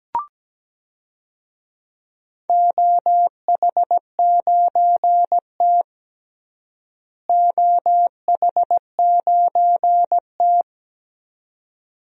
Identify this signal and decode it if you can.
Morse code, heard as a keyed tone, about 17 wpm, 700 Hz. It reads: OH9T OH9T
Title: DXCC-100 Call Signs - 4 Characters - Repeated 2x - 17wpm